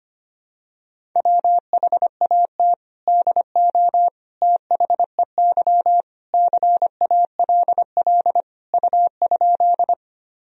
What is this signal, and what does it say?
Morse code, heard as a keyed tone, about 25 wpm, 700 Hz. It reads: WHAT DO THEY CALL U?
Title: Sentences from Top 100 Words and Abbreviations 25wpm